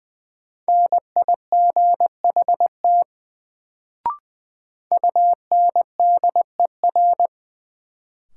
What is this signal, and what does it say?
Morse code, heard as a keyed tone, about 20 wpm, 700 Hz. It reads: NIGHT UNDER
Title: Sentences from Top 300 Words - New Word Drill - 20wpm